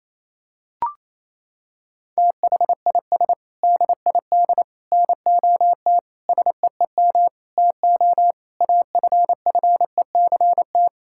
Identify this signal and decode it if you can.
Morse code, heard as a keyed tone, about 28 wpm, 700 Hz. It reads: THIS DID NOT SEEM TO AFFECT
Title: Sentences from Top 1000 Words - Review Part 6 - 28wpm